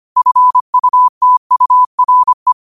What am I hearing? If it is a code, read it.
Morse code, heard as a keyed tone, about 25 wpm, 1000 Hz. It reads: FUTURE
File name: qotdfiles / 25wpm